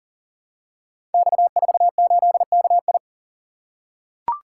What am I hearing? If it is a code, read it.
Morse code, heard as a keyed tone, about 42 wpm, 690 Hz. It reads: X48KI
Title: Code Groups - 5 Letter:Numbers - 40wpm